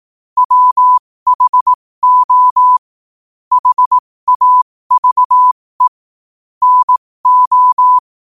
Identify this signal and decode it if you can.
Morse code, heard as a keyed tone, about 18 wpm, spaced as for 15 wpm, 1000 Hz. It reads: WHO HAVE NO
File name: qotdfiles / 15wpm